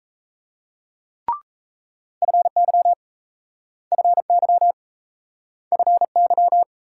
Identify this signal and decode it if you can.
Morse code, heard as a keyed tone, about 38 wpm, 700 Hz. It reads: FY FY FY